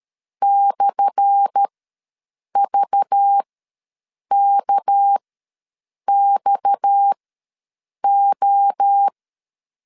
Morse code, heard as a keyed tone, about 13 wpm, spaced as for 7 wpm, 800 Hz. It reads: /VKXO
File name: LEZIONI / 10 SIGNS / 10A SLASH ws 12wpm 3443